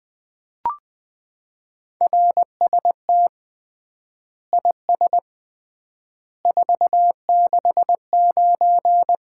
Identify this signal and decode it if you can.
Morse code, heard as a keyed tone, about 20 wpm, 700 Hz. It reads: RST IS 469
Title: Common QSO Phrases - 3x Word-Spacing - 20wpm